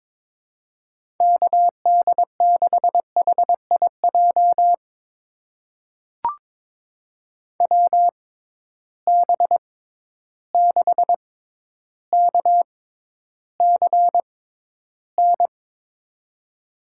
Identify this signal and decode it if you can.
Morse code, heard as a keyed tone, about 21 wpm, spaced as for 11 wpm, 700 Hz. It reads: KD6HIJ WB6KCN
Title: USA Call Signs - 6 Characters - 6x Character Spacing - 22wpm